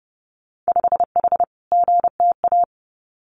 Morse code, heard as a keyed tone, about 30 wpm, 700 Hz. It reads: 5H GTA
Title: Winter Field Day Exchanges Encore 30wpm